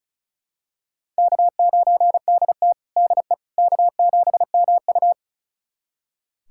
Morse code, heard as a keyed tone, about 36 wpm, 700 Hz. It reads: K9DT DE K7MU
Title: USA Callsign DE Callsign - 4 Characters - 35wpm